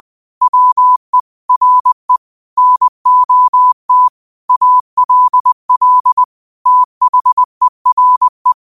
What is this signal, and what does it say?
Morse code, heard as a keyed tone, about 20 wpm, 1000 Hz. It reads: WE RE NOT ALL THERE
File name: qotdfiles / 20wpm